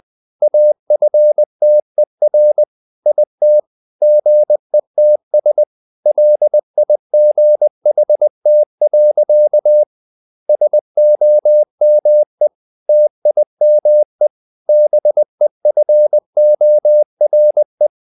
Morse code, heard as a keyed tone, about 20 wpm, 600 Hz. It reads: AFTER IT GETS LIGHT. SOME TIME BEFORE
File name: the old man and the sea / the old man and the sea part 1 20wpm